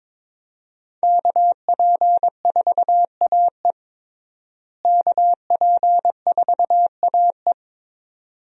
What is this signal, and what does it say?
Morse code, heard as a keyed tone, about 22 wpm, 700 Hz. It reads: KP4AE KP4AE
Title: USA Call Signs - 5 Characters - Repeated 2x - 22wpm